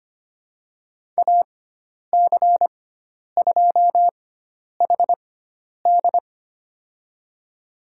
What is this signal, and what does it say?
Morse code, heard as a keyed tone, about 25 wpm, spaced as for 10 wpm, 700 Hz. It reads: AC2HD
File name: Lesson 16g - Letter c - Call Signs 25wpm